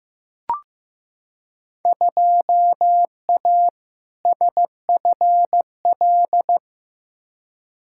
Morse code, heard as a keyed tone, about 15 wpm, 700 Hz. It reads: 2A SFL
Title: Field Day Exchanges Encore - 15wpm